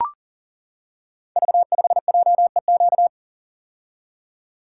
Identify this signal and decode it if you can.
Morse code, heard as a keyed tone, about 39 wpm, 690 Hz. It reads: V5JEQ
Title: Code Groups - 5 Letter:Numbers - 40wpm